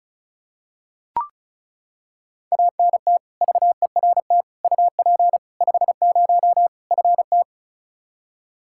Morse code, heard as a keyed tone, about 34 wpm, 700 Hz. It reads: ANT VERT UP 50 FT
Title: Practice QSO - Exchange 2 - Info - 35wpm